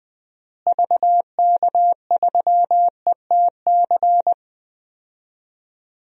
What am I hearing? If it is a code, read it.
Morse code, heard as a keyed tone, about 20 wpm, 700 Hz. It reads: VK3ETC
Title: DXCC-100 Call Signs - 6 Characters - Repeated 3x - 20wpm